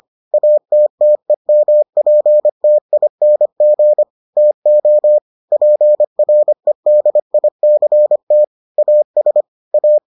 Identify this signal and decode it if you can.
Morse code, heard as a keyed tone, about 25 wpm, 600 Hz. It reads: ATTEMPTING TO PREDICT AS A